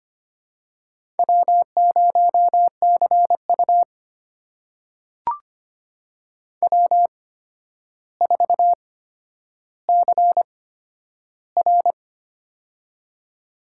Morse code, heard as a keyed tone, about 25 wpm, spaced as for 11 wpm, 700 Hz. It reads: W0CU W4CR
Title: USA Call Signs - 4 Characters - 8x Character Spacing - 25wpm